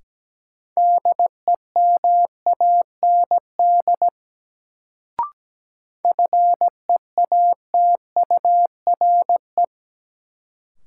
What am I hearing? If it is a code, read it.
Morse code, heard as a keyed tone, about 17 wpm, 700 Hz. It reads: DEMAND FEATURE
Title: Sentences from Top 800 Words - New Word Drill - 17wpm